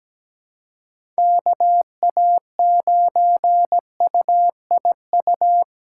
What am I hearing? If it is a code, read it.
Morse code, heard as a keyed tone, about 17 wpm, 700 Hz. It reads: KA9UIU
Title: USA Callsign DE Callsign - 6 Characters - 17wpm